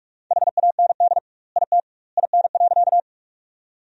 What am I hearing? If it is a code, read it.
Morse code, heard as a keyed tone, about 46 wpm, 700 Hz. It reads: HAND IT IN.
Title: 2x Word-Spacing - Sentences from Top 200 Words - 45wpm